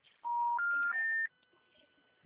Special information tone.